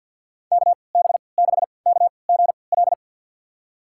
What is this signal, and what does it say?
Morse code, heard as a keyed tone, about 53 wpm, spaced as for 29 wpm, 700 Hz. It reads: KB6KGL
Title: USA Call Signs - 3x Character Spacing - 50wpm